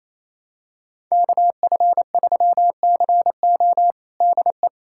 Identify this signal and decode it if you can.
Morse code, heard as a keyed tone, about 28 wpm, 700 Hz. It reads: KF3CO DE
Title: USA Callsign DE Callsign 28wpm